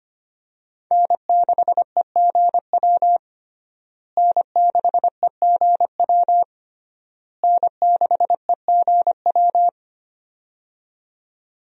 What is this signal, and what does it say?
Morse code, heard as a keyed tone, about 25 wpm, 700 Hz. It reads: N6EGW N6EGW N6EGW
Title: USA Call Signs - 5 Characters - Repeated 3x - 25wpm